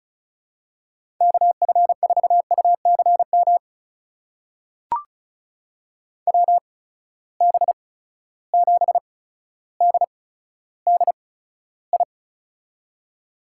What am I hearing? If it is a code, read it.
Morse code, heard as a keyed tone, about 36 wpm, spaced as for 15 wpm, 700 Hz. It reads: KF4UCM WB7DDI
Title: USA Call Signs - 6 Characters - 8x Character Spacing - 35wpm